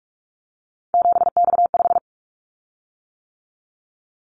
Morse code, heard as a keyed tone, about 44 wpm, 700 Hz. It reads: 7X5
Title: Callsign Prefixes for Top 100 DXCC - 45wpm